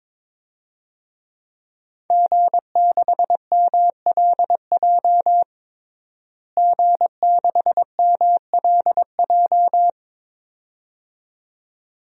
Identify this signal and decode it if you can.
Morse code, heard as a keyed tone, about 22 wpm, 700 Hz. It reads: G6MLJ G6MLJ
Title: DXCC-100 Call Signs - 5 Characters - Repeated 2x - 22wpm